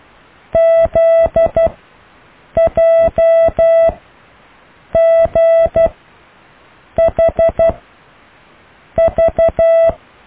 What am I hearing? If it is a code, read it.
Morse code, heard as a keyed tone, about 12 wpm, spaced as for 6 wpm, 660 Hz. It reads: ZJGHV
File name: cw / Alle letters